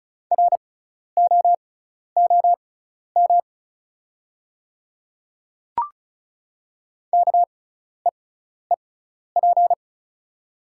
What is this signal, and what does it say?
Morse code, heard as a keyed tone, about 34 wpm, spaced as for 12 wpm, 700 Hz. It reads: ROOM KEEP
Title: 6x Character Spacing - Top 300 Words - 35wpm